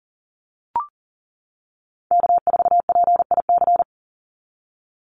Morse code, heard as a keyed tone, about 40 wpm, 700 Hz. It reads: K4PIC
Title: NA SOTA Chaser Callsigns 40wpm